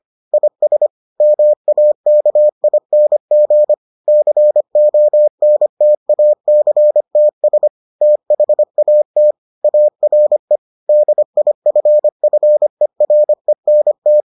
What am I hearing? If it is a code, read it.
Morse code, heard as a keyed tone, about 25 wpm, 600 Hz. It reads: IS MAKING CONTACTS THAT ARE DIFFERENT